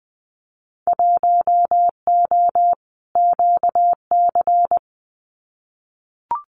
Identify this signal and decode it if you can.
Morse code, heard as a keyed tone, about 20 wpm, 700 Hz. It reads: 1O QC
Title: Winter Field Day Exchanges 20wpm